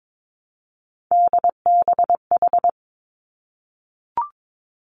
Morse code, heard as a keyed tone, about 22 wpm, 700 Hz. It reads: DBH
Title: Three Challenging Letter-Number Combinations 22wpm